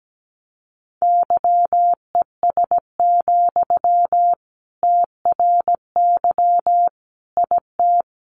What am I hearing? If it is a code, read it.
Morse code, heard as a keyed tone, about 17 wpm, 700 Hz. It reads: YES, TRY IT
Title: Sentences from Top 600 Words 17wpm